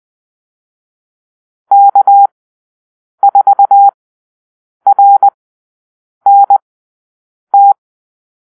Morse code, heard as a keyed tone, about 20 wpm, spaced as for 7 wpm, 800 Hz. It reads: K4RNT
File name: arrl / Call-Sign Drill-10